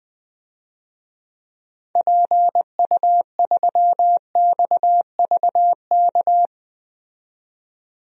Morse code, heard as a keyed tone, about 20 wpm, 700 Hz. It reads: PU3XVK
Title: DXCC-100 Call Signs - 6 Characters - 20wpm